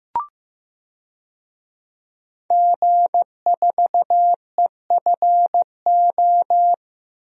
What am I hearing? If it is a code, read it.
Morse code, heard as a keyed tone, about 15 wpm, 700 Hz. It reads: G4EFO
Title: DXCC-100 Call Signs - 5 Characters - 15wpm